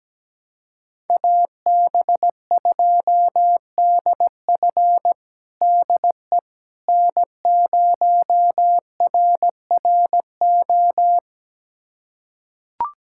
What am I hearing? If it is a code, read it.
Morse code, heard as a keyed tone, about 17 wpm, 700 Hz. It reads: AB2DF DE N0RRO
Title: USA Callsign DE Callsign 17wpm